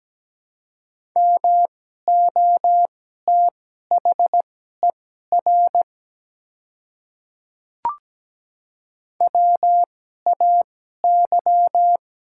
Morse code, heard as a keyed tone, about 17 wpm, spaced as for 12 wpm, 700 Hz. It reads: MOTHER WAY
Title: 2x Character Spacing - Top 200 Words - 17wpm